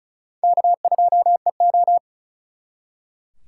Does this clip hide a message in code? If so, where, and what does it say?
Yes: Morse code from 0.4 to 2.0 s, heard as a keyed tone, about 46 wpm, 700 Hz. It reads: K2EO